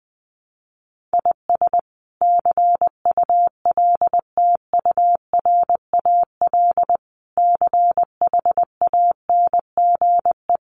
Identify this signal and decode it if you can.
Morse code, heard as a keyed tone, about 19 wpm, 700 Hz. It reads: IS CULTURAL CHANGE
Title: Sentences from Top 1500 Words 20wpm